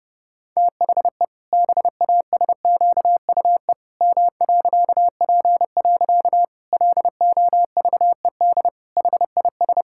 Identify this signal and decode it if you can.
Morse code, heard as a keyed tone, about 30 wpm, 700 Hz. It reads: THE BASQUE M.P. LOVED HIS
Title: Pangram Challenge v2 - 30wpm